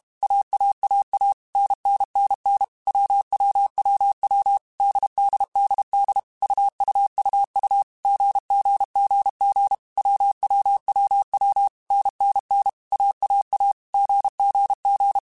Morse code, heard as a keyed tone, about 32 wpm, 800 Hz. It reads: AAAA NNNN WWWW DDDD UUUU GGGG WWWW NNN AAA GGG
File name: Basic Sets / Basic Letter Groups Collective7 / AWUNDG@30